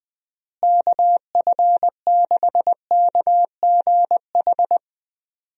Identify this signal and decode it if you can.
Morse code, heard as a keyed tone, about 20 wpm, 700 Hz. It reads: KF6KGH
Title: USA Call Signs - 6 Characters - Repeated 3x - 20wpm